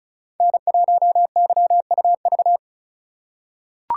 Morse code, heard as a keyed tone, about 34 wpm, 700 Hz. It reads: N1YUV
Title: USA Call Signs - 5 Characters - 2x Character Spacing - 35wpm